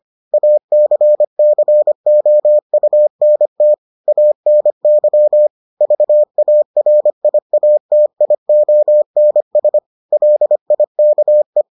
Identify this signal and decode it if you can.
Morse code, heard as a keyed tone, about 25 wpm, 600 Hz. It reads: ACCOUNT ANY VARIATIONS LIKE